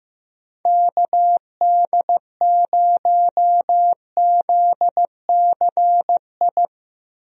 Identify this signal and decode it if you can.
Morse code, heard as a keyed tone, about 15 wpm, 700 Hz. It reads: KD0ZCI
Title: USA Call Signs - 6 Characters - Repeated 3x - 15wpm